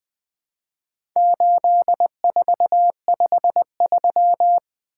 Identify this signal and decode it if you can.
Morse code, heard as a keyed tone, about 20 wpm, 700 Hz. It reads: 8453
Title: Four Letter-Number Combinations - 20wpm